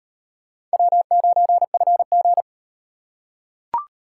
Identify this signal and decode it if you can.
Morse code, heard as a keyed tone, about 38 wpm, 700 Hz. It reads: W9FG